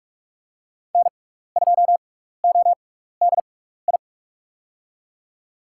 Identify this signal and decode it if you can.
Morse code, heard as a keyed tone, about 44 wpm, spaced as for 15 wpm, 700 Hz. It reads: N2ODI
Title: USA Call Signs - 6x Character Spacing - 45wpm